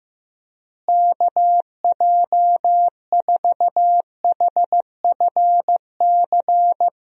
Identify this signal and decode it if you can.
Morse code, heard as a keyed tone, about 15 wpm, 700 Hz. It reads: KJ4HFC
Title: USA Call Signs - 6 Characters - Repeated 3x - 15wpm